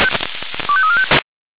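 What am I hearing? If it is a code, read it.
Five-tone selective call (ZVEI), heard as quick runs of five tones, tones of about 70 ms. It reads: 25345
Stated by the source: Five-tone ZVEI selective call received by SDR.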